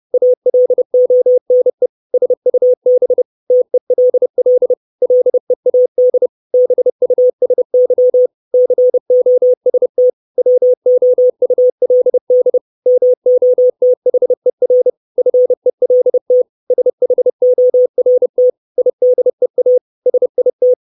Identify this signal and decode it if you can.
Morse code, heard as a keyed tone, about 30 wpm, 500 Hz. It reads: ALONE SUB TELL LEAD BUSY COST WOULD MOTHER FELT SHORT IDEA SIT
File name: 632 Words 30WPM